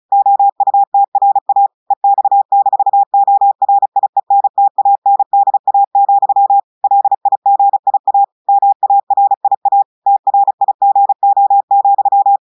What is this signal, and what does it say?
Morse code, heard as a keyed tone, about 36 wpm, 800 Hz. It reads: OUTRA EX-ORIENTANDA, LIGIA MARIA TRIGO,
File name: audios / portugais / portugais texte en clair 1 vitesse 35 mots minutes